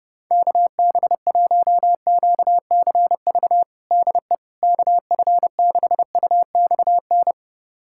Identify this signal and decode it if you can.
Morse code, heard as a keyed tone, about 30 wpm, 700 Hz. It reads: KB1QCV DE KF6UXN
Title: USA Callsign DE Callsign - 6 Characters - 30wpm